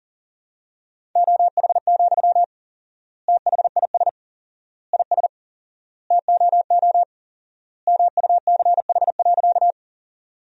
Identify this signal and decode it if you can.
Morse code, heard as a keyed tone, about 40 wpm, 700 Hz. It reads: OH, THIS IS TOO MUCH.
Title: 4x Word-Spacing - Sentences from Top 500 Words - 40wpm